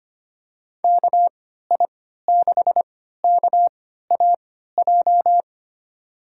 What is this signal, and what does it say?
Morse code, heard as a keyed tone, about 25 wpm, spaced as for 14 wpm, 700 Hz. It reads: KI6KAJ
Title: USA Call Signs - 3x Character Spacing - 25wpm